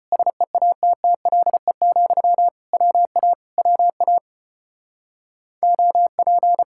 Morse code, heard as a keyed tone, about 33 wpm, 700 Hz. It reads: SEATTLE, WA WA OP